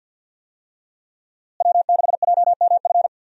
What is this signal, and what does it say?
Morse code, heard as a keyed tone, about 49 wpm, 700 Hz. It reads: W6JMF
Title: USA Call Signs - 5 Characters - Repeated 3x - 50wpm